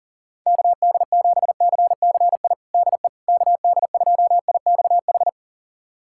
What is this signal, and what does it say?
Morse code, heard as a keyed tone, about 40 wpm, 700 Hz. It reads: KD7CCI DE KD2IXH
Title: USA Callsign DE Callsign - 6 Characters - 40wpm